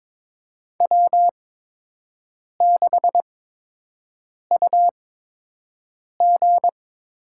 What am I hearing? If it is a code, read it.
Morse code, heard as a keyed tone, about 22 wpm, spaced as for 6 wpm, 700 Hz. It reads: W6UG
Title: USA Call Signs - 4 Characters - 8x Character Spacing - 22wpm